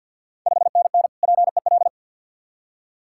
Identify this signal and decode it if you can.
Morse code, heard as a keyed tone, about 50 wpm, 690 Hz. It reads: 5NN PEL
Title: Ontario QSO Party Exchange 50wpm